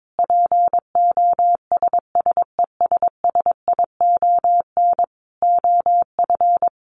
Morse code, heard as a keyed tone, about 22 wpm, 700 Hz. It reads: POSSESSION OF
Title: Sentences from Top 1400 Words 22wpm